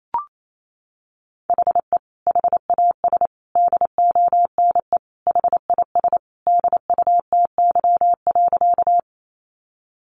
Morse code, heard as a keyed tone, about 28 wpm, 700 Hz. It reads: HE HAS DONE HIS DUTY.
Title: Sentences from Top 1500 Words 28wpm